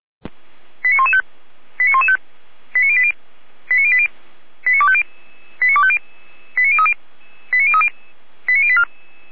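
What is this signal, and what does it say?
Five-tone selective call (ZVEI), heard as quick runs of five tones, tones of about 70 ms. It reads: 89115 89115 89080 89080 89250 89250 89030 89030 89084
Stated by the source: Five-tone ZVEI selective call received by SDR, fire brigade Orlamünde (feuerwehr-orlamuende).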